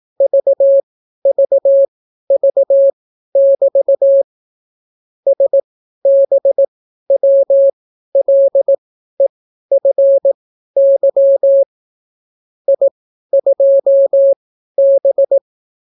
Morse code, heard as a keyed tone, about 18 wpm, spaced as for 12 wpm, 550 Hz. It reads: VVV<BT> SBWLEFY I2B